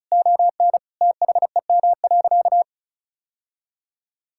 Morse code, heard as a keyed tone, about 36 wpm, 700 Hz. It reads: ON THEM.
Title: Sentences from Top 1600 Words 35wpm